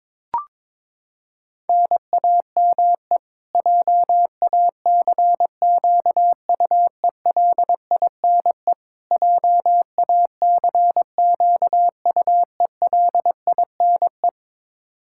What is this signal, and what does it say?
Morse code, heard as a keyed tone, about 22 wpm, 700 Hz. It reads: NAME JACQUELINE JACQUELINE
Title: Practice QSO - Exchange 1 - Beginning - 22wpm